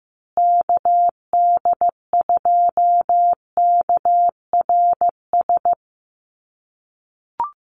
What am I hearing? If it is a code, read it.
Morse code, heard as a keyed tone, about 15 wpm, 700 Hz. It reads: KD2KRS
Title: USA Call Signs Repeated 2x 15wpm